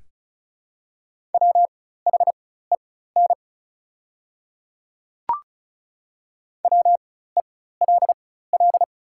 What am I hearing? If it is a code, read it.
Morse code, heard as a keyed tone, about 36 wpm, spaced as for 16 wpm, 700 Hz. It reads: WHEN WELL
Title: 4x Character Spacing - Top 200 Words - 35wpm